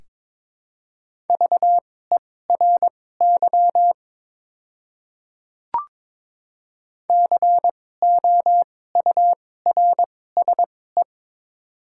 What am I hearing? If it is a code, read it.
Morse code, heard as a keyed tone, about 22 wpm, spaced as for 16 wpm, 700 Hz. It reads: VERY COURSE
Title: 2x Character Spacing - Top 500 Words - 22wpm